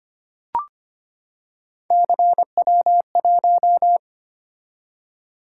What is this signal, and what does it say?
Morse code, heard as a keyed tone, about 25 wpm, 700 Hz. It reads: CW1
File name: Three Letter-Number Combinations - 25wpm